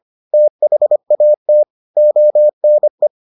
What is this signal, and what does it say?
Morse code, heard as a keyed tone, about 25 wpm, 600 Hz. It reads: THAT ONE